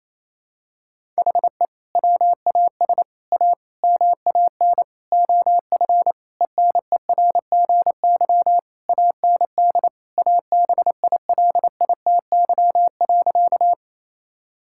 Morse code, heard as a keyed tone, about 28 wpm, 700 Hz. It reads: HE WAS A MAN OF ENERGY AND ABILITY.
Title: Sentences from Top 1000 Words - Review Part 6 - 28wpm